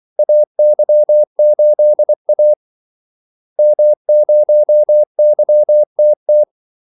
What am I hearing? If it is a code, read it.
Morse code, heard as a keyed tone, about 24 wpm, 600 Hz. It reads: AY8A M0YTT
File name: Callsigns-24wpm